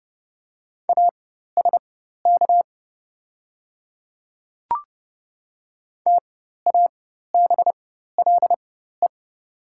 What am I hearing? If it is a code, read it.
Morse code, heard as a keyed tone, about 30 wpm, spaced as for 14 wpm, 690 Hz. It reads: ASK TABLE
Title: 4x Character Spacing - Top 500 Words - 30wpm